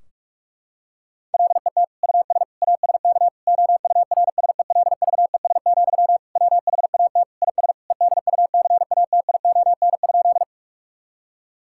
Morse code, heard as a keyed tone, about 43 wpm, 700 Hz. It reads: LET US ASK OURSELVES, WHAT IS EDUCATION?
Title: Sentences from Top 1600 Words 45wpm